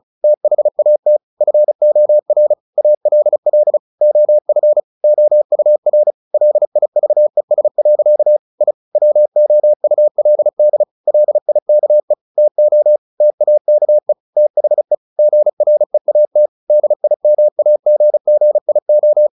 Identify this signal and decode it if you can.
Morse code, heard as a keyed tone, about 34 wpm, 600 Hz. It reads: THAT FOR ALL OF OUR LIVES. I WOULD LIKE TO TAKE THE GREAT DIMAGGIO
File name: the old man and the sea / the old man and the sea 1 35wpm